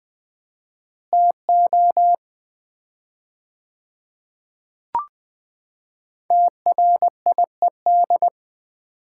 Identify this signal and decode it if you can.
Morse code, heard as a keyed tone, about 20 wpm, 700 Hz. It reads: TO TRIED
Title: 4x Word-Spacing - Sets of 2 Words - 20wpm